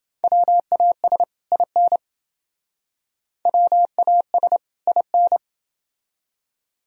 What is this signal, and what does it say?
Morse code, heard as a keyed tone, about 28 wpm, 700 Hz. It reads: WAS IN WAS IN